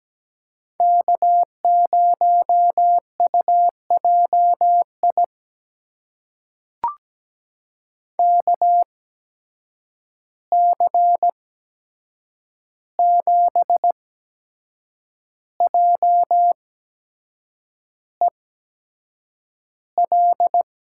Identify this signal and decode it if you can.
Morse code, heard as a keyed tone, about 17 wpm, spaced as for 7 wpm, 700 Hz. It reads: K0UJI KC7JEL
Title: USA Call Signs - 8x Character Spacing - 17wpm